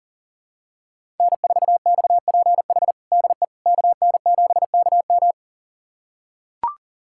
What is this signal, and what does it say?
Morse code, heard as a keyed tone, about 40 wpm, 700 Hz. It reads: N4XPH DE KN7KM
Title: USA Callsign DE Callsign - 5 Characters - 40wpm